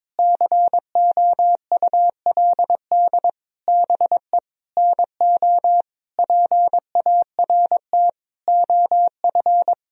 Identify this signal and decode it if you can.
Morse code, heard as a keyed tone, about 22 wpm, 700 Hz. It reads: COULD BE NO PART OF
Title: Sentences from Top 1000 Words - Review Part 2 - 22wpm